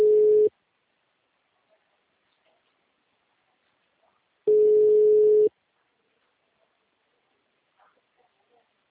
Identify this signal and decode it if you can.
Ringback tone (Europe, 425 Hz).
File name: out- -2012- - - .4761